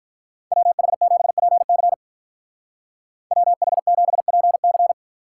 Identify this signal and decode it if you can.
Morse code, heard as a keyed tone, about 43 wpm, 700 Hz. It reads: WH7PC WH7PC
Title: Speed Racing - USA 2x2 Call Signs 35wpm